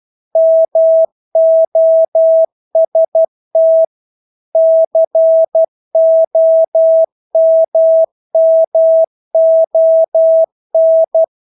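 Morse code, heard as a keyed tone, about 12 wpm, 650 Hz. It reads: MOST COMMON
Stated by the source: Morse practice of English words at 12 wpm.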